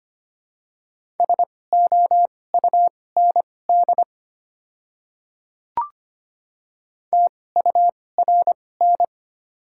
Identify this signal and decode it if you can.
Morse code, heard as a keyed tone, about 25 wpm, spaced as for 18 wpm, 700 Hz. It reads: SOUND TURN
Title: 2x Character Spacing - Top 200 Words - 25wpm